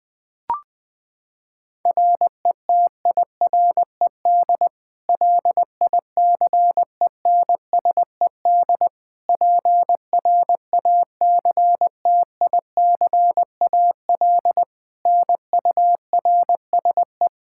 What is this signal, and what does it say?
Morse code, heard as a keyed tone, about 21 wpm, 700 Hz. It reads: RETIRED LICENSED PRACTICAL NURSE
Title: Practice QSO - Exchange 3 - More Info - 20wpm